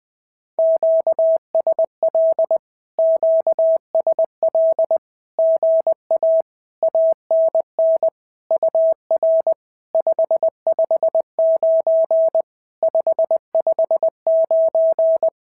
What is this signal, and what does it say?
Morse code, heard as a keyed tone, about 20 wpm, 650 Hz. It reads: QSL QSL GA ANN UR 559 559
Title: SOTA QSO Practice 20wpm